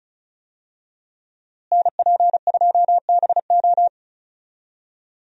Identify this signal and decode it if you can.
Morse code, heard as a keyed tone, about 36 wpm, 700 Hz. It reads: NP2BO
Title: USA Call Signs - 5 Characters - Repeated 2x - 35wpm